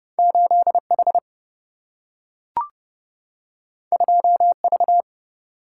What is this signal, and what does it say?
Morse code, heard as a keyed tone, about 30 wpm, 700 Hz. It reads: 8 H 2 V
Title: Single-Letter Single-Number 30wpm